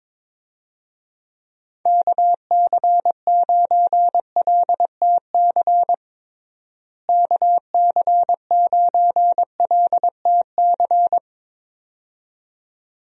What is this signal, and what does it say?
Morse code, heard as a keyed tone, about 22 wpm, 700 Hz. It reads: KC9LTC KC9LTC
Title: USA Call Signs - 6 Characters - Repeated 2x - 22wpm